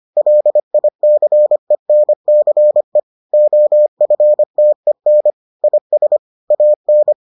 Morse code, heard as a keyed tone, about 25 wpm, 600 Hz. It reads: LICENCE OFTEN IS AN